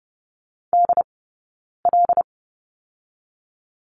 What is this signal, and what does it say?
Morse code, heard as a keyed tone, about 30 wpm, 700 Hz. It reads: D L
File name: Lesson 20d - Number 3 - Two Characters 30wpm